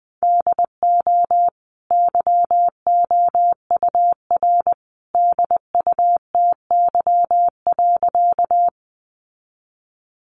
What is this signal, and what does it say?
Morse code, heard as a keyed tone, about 20 wpm, 700 Hz. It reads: DO YOUR DUTY.
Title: Sentences from Top 1500 Words 20wpm